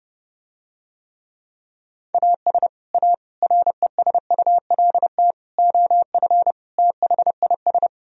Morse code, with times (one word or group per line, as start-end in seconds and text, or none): 2.1-2.7 AS
2.9-3.1 A
3.4-5.3 RESULT
5.6-6.5 OF
6.8-7.9 THIS